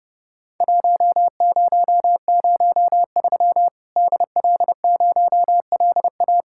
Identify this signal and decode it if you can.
Morse code, heard as a keyed tone, about 32 wpm, 700 Hz. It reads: 1003 DL0LA
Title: Worked All Europe DX Contest - QTC Exchange - 30wpm